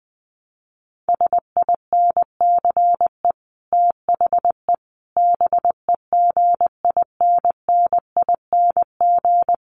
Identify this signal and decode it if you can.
Morse code, heard as a keyed tone, about 20 wpm, 700 Hz. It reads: SINCE THE BEGINNING
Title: Sets of 5 Words Encore2 20wpm